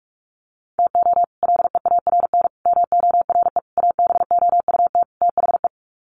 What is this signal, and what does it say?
Morse code, heard as a keyed tone, about 44 wpm, 700 Hz. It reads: TO LEARN MORE ABOUT THE
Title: Sets of 5 Words Encore2 45wpm